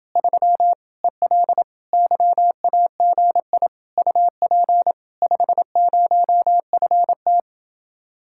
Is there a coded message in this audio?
Yes: Morse code from 0.2 to 7.4 s, heard as a keyed tone, about 29 wpm, 700 Hz. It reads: 3 EL YAGI UP 50FT